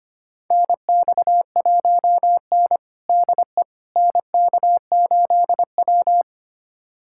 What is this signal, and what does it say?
Morse code, heard as a keyed tone, about 25 wpm, 700 Hz. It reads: NX1N DE NK8W
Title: USA Callsign DE Callsign - 4 Characters - 25wpm